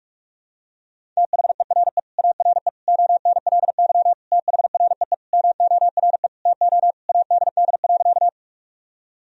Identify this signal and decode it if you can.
Morse code, heard as a keyed tone, about 46 wpm, 700 Hz. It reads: THERE ARE ONLY THREE MORE TO ADD.
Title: Sentences from Top 1000 Words - Review Part 1 - 45wpm